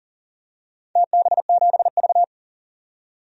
Morse code, heard as a keyed tone, about 39 wpm, 700 Hz. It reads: TB7V
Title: DXCC-100 Call Signs - 4 Characters - 40wpm